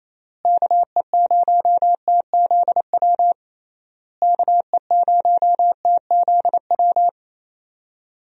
Morse code, heard as a keyed tone, about 27 wpm, 700 Hz. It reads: KE0TZW KE0TZW
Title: USA Call Signs - 6 Characters - Repeated 3x - 28wpm